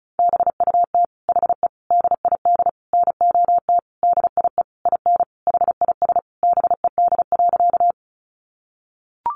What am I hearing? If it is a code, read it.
Morse code, heard as a keyed tone, about 34 wpm, 700 Hz. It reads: BUT HE DID NOT DIE IN HIS BED.
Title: Sentences from Top 1300 Words 35wpm